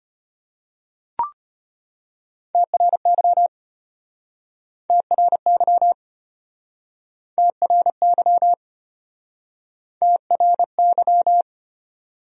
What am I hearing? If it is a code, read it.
Morse code, heard as a keyed tone, about 37 wpm, 700 Hz. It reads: TRY TRY TRY TRY